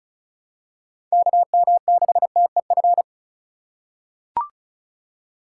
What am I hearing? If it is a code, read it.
Morse code, heard as a keyed tone, about 36 wpm, 700 Hz. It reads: KM6TEF
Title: USA Call Signs - 6 Characters - 4x Character Spacing - 35wpm